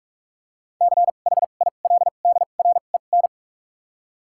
Morse code, heard as a keyed tone, about 47 wpm, spaced as for 33 wpm, 700 Hz. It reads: CHILDREN
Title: 2x Character Spacing - Top 300 Words - 45wpm